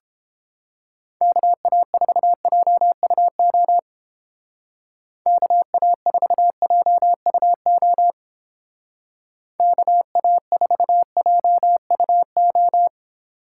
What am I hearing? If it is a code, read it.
Morse code, heard as a keyed tone, about 29 wpm, 700 Hz. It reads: KA4JUO KA4JUO KA4JUO